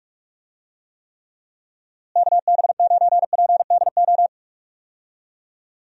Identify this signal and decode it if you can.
Morse code, heard as a keyed tone, about 42 wpm, 700 Hz. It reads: KB9PDO
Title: USA Call Signs - 6 Characters - Repeated 3x - 45wpm